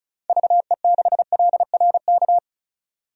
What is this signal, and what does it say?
Morse code, heard as a keyed tone, about 34 wpm, 700 Hz. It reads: VE6LRK
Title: DXCC-100 Call Signs - 6 Characters - 35wpm